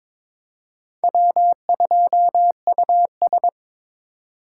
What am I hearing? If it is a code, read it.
Morse code, heard as a keyed tone, about 26 wpm, 700 Hz. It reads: W2US
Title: Speed Racing - USA 1x2 Call Signs 22wpm